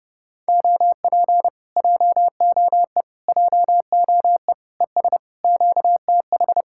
Morse code, heard as a keyed tone, about 28 wpm, 700 Hz. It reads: OP JOE JOE ES QTH
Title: Speed-Racing Common QSO Phrases - 30wpm